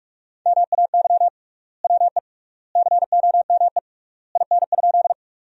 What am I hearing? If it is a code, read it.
Morse code, heard as a keyed tone, about 44 wpm, 700 Hz. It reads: MAY WE COME IN?
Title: 3x Word-Spacing - Sentences from Top 100 Words - 45wpm